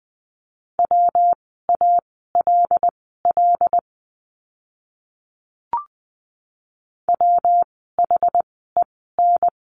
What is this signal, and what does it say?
Morse code, heard as a keyed tone, about 20 wpm, spaced as for 15 wpm, 700 Hz. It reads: WALL WHEN
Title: 115 Lesson 19f - Letter w - Words 20wpm